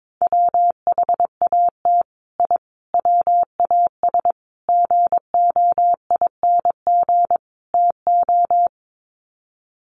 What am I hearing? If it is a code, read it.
Morse code, heard as a keyed tone, about 22 wpm, 700 Hz. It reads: WHAT I WAS GOING TO
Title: Sets of 5 Words Encore2 22wpm